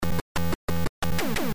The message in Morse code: none